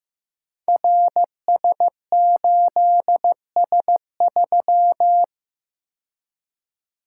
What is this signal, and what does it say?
Morse code, heard as a keyed tone, about 15 wpm, 700 Hz. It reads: RS8S3
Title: Code Groups - 5 Letter:Numbers - 15wpm